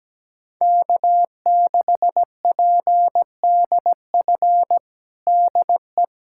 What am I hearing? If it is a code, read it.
Morse code, heard as a keyed tone, about 17 wpm, 700 Hz. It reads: K6PDF DE
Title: USA Callsign DE Callsign - 5 Characters - 17wpm